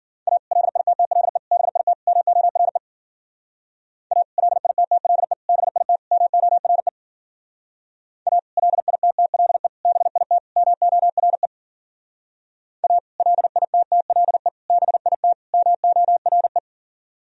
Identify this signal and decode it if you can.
Morse code, heard as a keyed tone, about 52 wpm, 700 Hz. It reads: A LITTLE BIT MORE A LITTLE BIT MORE A LITTLE BIT MORE A LITTLE BIT MORE